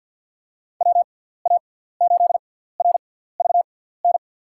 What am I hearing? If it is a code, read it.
Morse code, heard as a keyed tone, about 52 wpm, spaced as for 17 wpm, 700 Hz. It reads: WA8RVN
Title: USA Call Signs - 6 Characters - 6x Character Spacing - 50wpm